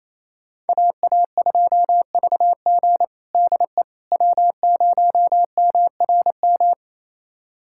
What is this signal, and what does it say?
Morse code, heard as a keyed tone, about 28 wpm, 700 Hz. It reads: AA2VG DE W0MRM
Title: USA Callsign DE Callsign - 5 Characters - 28wpm